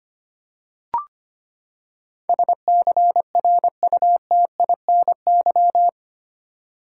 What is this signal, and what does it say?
Morse code, heard as a keyed tone, about 25 wpm, 700 Hz. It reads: SCRUTINY
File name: Eight Letter Words - 25wpm